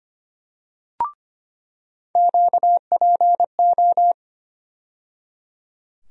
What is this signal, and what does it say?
Morse code, heard as a keyed tone, about 25 wpm, 700 Hz. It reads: QPO
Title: USA Call Sign Postfixes - 3 Letters 25wpm